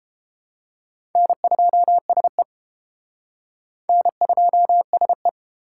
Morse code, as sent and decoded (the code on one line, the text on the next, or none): -. ..--- ... / -. ..--- ... .
N2SE N2SE